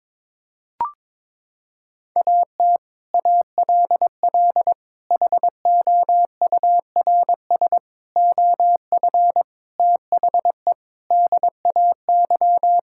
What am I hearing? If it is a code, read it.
Morse code, heard as a keyed tone, about 22 wpm, 700 Hz. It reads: AT ALL HOURS OF THE DAY
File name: practice / 22wpm / Sets of 8 Words - 22wpm